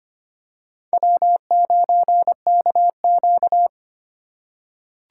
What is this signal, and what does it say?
Morse code, heard as a keyed tone, about 25 wpm, 700 Hz. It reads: W9KQ
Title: USA Call Signs - 4 Characters - 3x Character Spacing - 25wpm